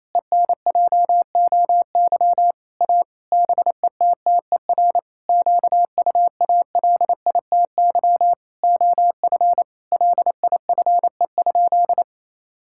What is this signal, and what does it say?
Morse code, heard as a keyed tone, about 28 wpm, 700 Hz. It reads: ENJOY A BETTER QUALITY OF LIFE?
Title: Sentences from Top 1000 Words - Review Part 5 - 28wpm